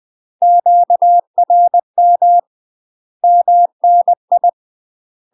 Morse code, heard as a keyed tone, about 20 wpm, 700 Hz. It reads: QRM MNI
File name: UK MASTER QSOs TRAINING / 20WPM / UK MASTER QSOs 0053